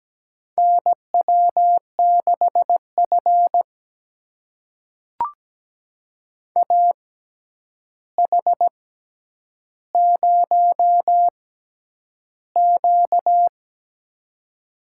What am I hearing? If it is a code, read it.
Morse code, heard as a keyed tone, about 17 wpm, spaced as for 9 wpm, 700 Hz. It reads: NW6F AH0Q
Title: USA Call Signs - 4 Characters - 6x Character Spacing - 17wpm